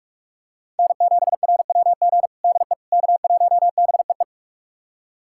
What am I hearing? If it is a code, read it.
Morse code, heard as a keyed tone, about 41 wpm, 700 Hz. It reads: N7RWG DE K1BEE
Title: USA Callsign DE Callsign - 5 Characters - 45wpm